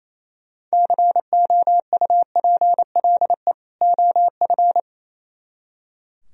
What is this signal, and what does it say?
Morse code, heard as a keyed tone, about 36 wpm, 700 Hz. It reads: COUPLE OF